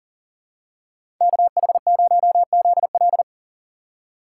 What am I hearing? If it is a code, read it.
Morse code, heard as a keyed tone, about 40 wpm, 700 Hz. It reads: KH0ZL
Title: USA Call Signs - 5 Characters - 6x Character Spacing - 40wpm